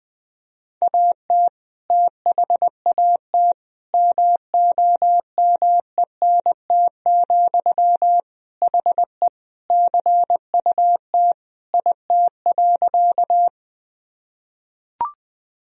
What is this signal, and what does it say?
Morse code, heard as a keyed tone, about 21 wpm, 700 Hz. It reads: AT THAT MOMENT, HE CUT IT.
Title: Sentences from Top 1000 Words - Review Part 1 - 20wpm